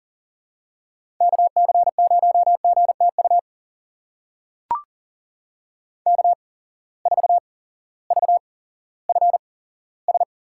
Morse code, heard as a keyed tone, about 40 wpm, spaced as for 18 wpm, 690 Hz. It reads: KC0GTU K4VFS
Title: USA Call Signs - 8x Character Spacing - 40wpm